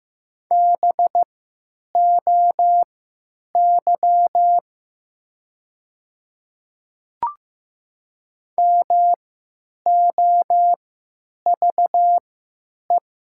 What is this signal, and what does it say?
Morse code, heard as a keyed tone, about 15 wpm, spaced as for 9 wpm, 700 Hz. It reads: BOY MOVE
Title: 3x Character Spacing - Top 200 Words - 15wpm